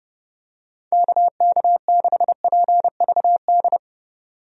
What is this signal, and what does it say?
Morse code, heard as a keyed tone, about 30 wpm, 700 Hz. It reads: KK6PVD
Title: USA Call Signs - 6 Characters - 30wpm